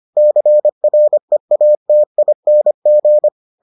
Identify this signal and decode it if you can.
Morse code, heard as a keyed tone, about 25 wpm, 600 Hz. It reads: CREATING